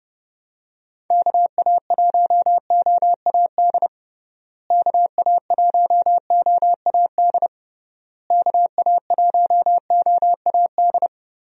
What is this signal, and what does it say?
Morse code, heard as a keyed tone, about 30 wpm, 700 Hz. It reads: KA1OAD KA1OAD KA1OAD
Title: USA Call Signs - 6 Characters - Repeated 3x - 30wpm